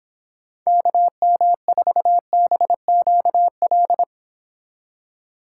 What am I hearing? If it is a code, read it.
Morse code, heard as a keyed tone, about 29 wpm, 700 Hz. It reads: KM4BQL